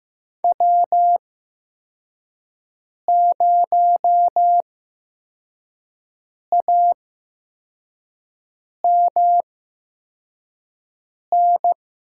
Morse code, heard as a keyed tone, about 15 wpm, spaced as for 4 wpm, 700 Hz. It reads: W0AMN
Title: USA Call Signs - 8x Character Spacing - 15wpm